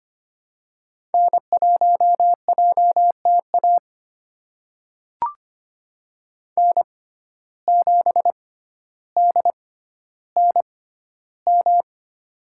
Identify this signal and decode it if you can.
Morse code, heard as a keyed tone, about 25 wpm, spaced as for 13 wpm, 700 Hz. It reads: N1JTA N7DNM
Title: USA Call Signs - 5 Characters - 6x Character Spacing - 25wpm